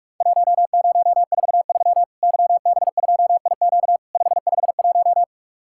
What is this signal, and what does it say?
Morse code, heard as a keyed tone, about 48 wpm, 700 Hz. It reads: 1043 YB2IQ 551
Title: Worked All Europe DX Contest - QTC Exchange - 45wpm